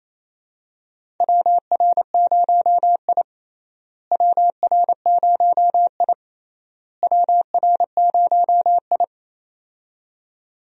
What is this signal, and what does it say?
Morse code, heard as a keyed tone, about 27 wpm, 700 Hz. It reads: WR0I WR0I WR0I
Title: USA Call Signs - 4 Characters - Repeated 3x - 28wpm